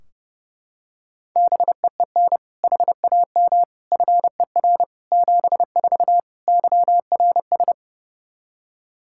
Morse code, heard as a keyed tone, about 30 wpm, 700 Hz. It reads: BEEN HAM FER 74 YRS
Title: Practice QSO - Exchange 3 - More Info - 30wpm